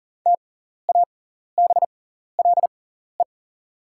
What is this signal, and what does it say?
Morse code, heard as a keyed tone, about 40 wpm, spaced as for 14 wpm, 690 Hz. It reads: TABLE
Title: 6x Character Spacing - Top 500 Words - 40wpm